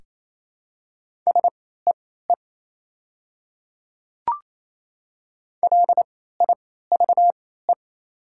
Morse code, heard as a keyed tone, about 28 wpm, spaced as for 16 wpm, 700 Hz. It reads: SEE LIVE
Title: 3x Character Spacing - Top 500 Words - 28wpm